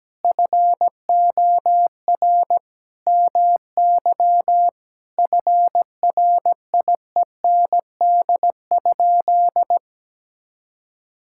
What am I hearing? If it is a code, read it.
Morse code, heard as a keyed tone, about 17 wpm, 700 Hz. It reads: FOR MY FRIEND?
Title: Sentences from Top 1000 Words - Review Part 5 - 17wpm